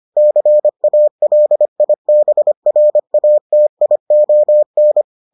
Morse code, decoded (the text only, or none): CALIBRATION